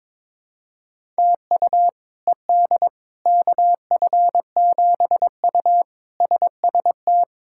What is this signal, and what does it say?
Morse code, heard as a keyed tone, about 22 wpm, 700 Hz. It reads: TU ED KF7U SST
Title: K1USN SST QSO Practice - 22wpm